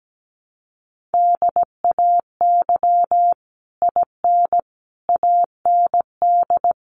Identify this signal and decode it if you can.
Morse code, heard as a keyed tone, about 17 wpm, 700 Hz. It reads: DAY IN AND
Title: Sets of 5 Words Encore2 17wpm